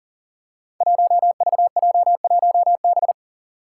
Morse code, heard as a keyed tone, about 40 wpm, 700 Hz. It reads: 1VJ1B
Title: Five Challenging Letter-Number Combinations 40wpm